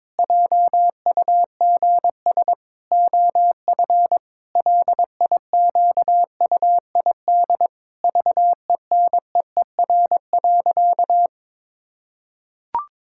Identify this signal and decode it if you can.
Morse code, heard as a keyed tone, about 22 wpm, 700 Hz. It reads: JUGS OF LIQUID VENEER.
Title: Pangram Challenge v2 - 22wpm